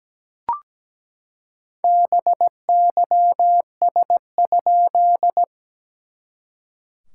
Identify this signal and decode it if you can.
Morse code, heard as a keyed tone, about 17 wpm, 690 Hz. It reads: BYS?
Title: SOTA QSO Practice 17wpm